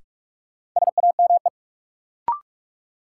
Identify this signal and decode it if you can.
Morse code, heard as a keyed tone, about 44 wpm, 700 Hz. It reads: SAME
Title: Sentences from Top 200 Words - New Word Drill - 45wpm